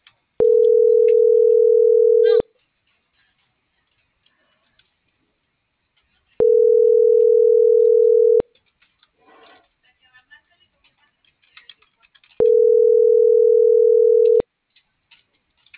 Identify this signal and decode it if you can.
Ringback tone (North America), 440 + 480 Hz.